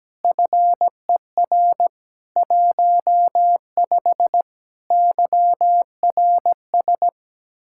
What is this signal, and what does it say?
Morse code, heard as a keyed tone, about 17 wpm, 700 Hz. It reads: FER 15 YRS
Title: K1USN SST QSO Practice 17wpm